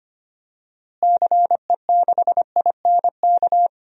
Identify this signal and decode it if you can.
Morse code, heard as a keyed tone, about 25 wpm, 700 Hz. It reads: CE6INK
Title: DXCC-100 Call Signs - 6 Characters - Repeated 2x - 25wpm